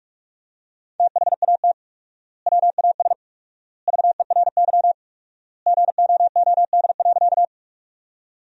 Morse code, heard as a keyed tone, about 45 wpm, 700 Hz. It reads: THAT WAS VERY GOOD.
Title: 4x Word-Spacing - Sentences from Top 200 Words - 45wpm